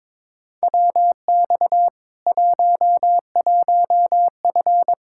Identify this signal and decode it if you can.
Morse code, heard as a keyed tone, about 21 wpm, 700 Hz. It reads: WX 11F
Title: Practice QSO - Exchange 2 - Info - 22wpm